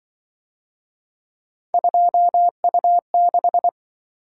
Morse code, heard as a keyed tone, about 25 wpm, 700 Hz. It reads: 2U6
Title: Speed-Racing - Three Letter-Number Combinations - 20wpm